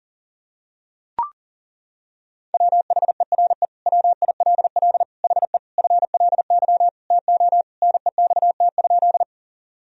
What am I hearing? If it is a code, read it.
Morse code, heard as a keyed tone, about 40 wpm, 700 Hz. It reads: WHERE WILL HE FLY TO NEXT?
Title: Sentences from Top 1000 Words - Review Part 2 - 40wpm